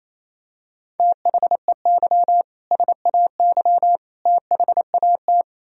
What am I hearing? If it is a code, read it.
Morse code, heard as a keyed tone, about 28 wpm, 700 Hz. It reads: THEY SAY THAT
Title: Sentences from Top 1000 Words - Review Part 3 - 28wpm